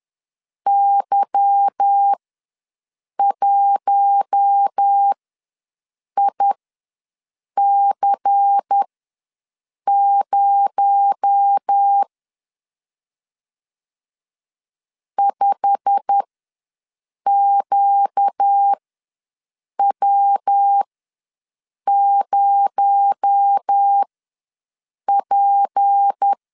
Morse code, heard as a keyed tone, about 10 wpm, spaced as for 6 wpm, 800 Hz. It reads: Y1IC0 5QW0P
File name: LEZIONI / 9 286 ws 10wpm 4032